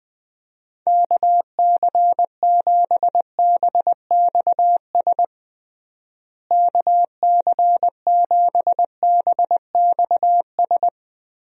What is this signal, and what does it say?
Morse code, heard as a keyed tone, about 20 wpm, 700 Hz. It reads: KC7BXS KC7BXS
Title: USA Call Signs - 6 Characters - Repeated 3x - 20wpm